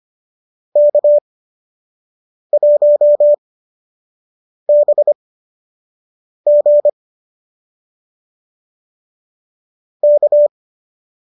Morse code, heard as a keyed tone, about 25 wpm, spaced as for 6 wpm, 590 Hz. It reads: K1BG K